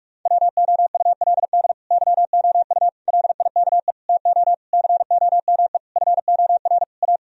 Morse code, heard as a keyed tone, about 42 wpm, 700 Hz. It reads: WOULD YOU LIKE TO COME FOR A
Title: Sentences from Top 1000 Words - Review Part 1 - 45wpm